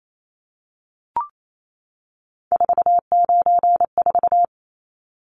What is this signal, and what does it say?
Morse code, heard as a keyed tone, about 28 wpm, 700 Hz. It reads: 494
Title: Numbers - 3-Digit 28wpm